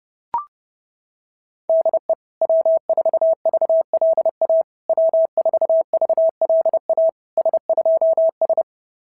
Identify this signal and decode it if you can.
Morse code, heard as a keyed tone, about 28 wpm, 650 Hz. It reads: DE W4VLA W4VLA S2S
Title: SOTA QSO Practice 30wpm